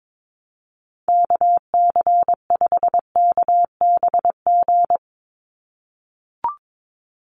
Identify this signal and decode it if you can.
Morse code, heard as a keyed tone, about 22 wpm, 700 Hz. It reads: KC5KBG
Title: USA Call Signs 22wpm